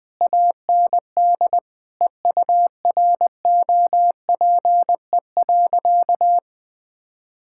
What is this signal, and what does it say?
Morse code, heard as a keyed tone, about 20 wpm, 700 Hz. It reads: AND EUROPE.
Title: Sentences with 5-Word Phrases - Spoken Beginning - 20wpm